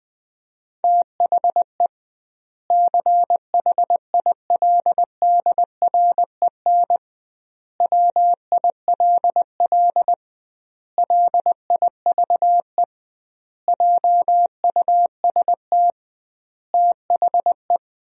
Morse code, heard as a keyed tone, about 20 wpm, 700 Hz. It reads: THE CHILDREN WILL LIVE JUST THE
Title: 2x Word-Spacing - Sentences from Top 300 Words - 20wpm